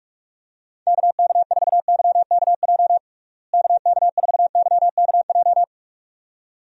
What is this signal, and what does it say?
Morse code, heard as a keyed tone, about 41 wpm, 700 Hz. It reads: KK4YKJ KK4YKJ
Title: USA Call Signs - 6 Characters - Repeated 2x - 45wpm